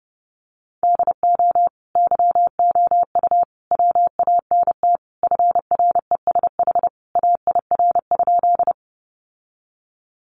Morse code, heard as a keyed tone, about 30 wpm, 700 Hz. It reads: DO YOU WANT FRESH AIR?
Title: Sentences from Top 1200 Words 30wpm